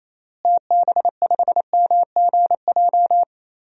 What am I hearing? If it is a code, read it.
Morse code, heard as a keyed tone, about 27 wpm, 700 Hz. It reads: TB5MGJ
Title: DXCC-100 Call Signs - 6 Characters - Repeated 3x - 28wpm